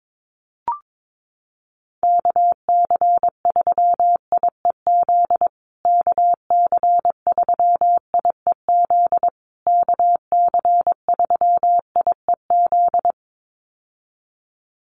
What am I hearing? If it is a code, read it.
Morse code, heard as a keyed tone, about 22 wpm, 700 Hz. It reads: KC3IEZ KC3IEZ KC3IEZ
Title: USA Call Signs Repeated 3x 22wpm